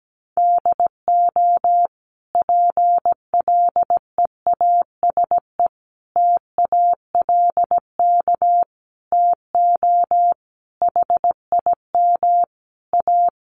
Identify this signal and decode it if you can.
Morse code, heard as a keyed tone, about 17 wpm, 700 Hz. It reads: DO PLEASE TALK TO HIM A
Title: Sentences from Top 1500 Words 17wpm